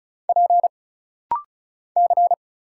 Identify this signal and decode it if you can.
Morse code, heard as a keyed tone, about 36 wpm, 700 Hz. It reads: P C
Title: Single Letter-Number Alt - Rapid-Fire 35wpm